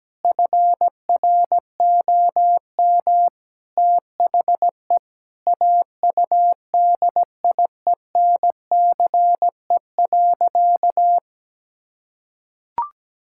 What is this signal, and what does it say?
Morse code, heard as a keyed tone, about 17 wpm, 700 Hz. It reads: FROM THE AUDIENCE.
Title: Sentences from Top 1000 Words - Review Part 5 - 17wpm